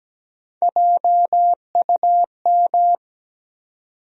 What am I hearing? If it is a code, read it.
Morse code, heard as a keyed tone, about 17 wpm, 700 Hz. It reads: JUM
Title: USA Call Sign Postfixes - 3 Letters 17wpm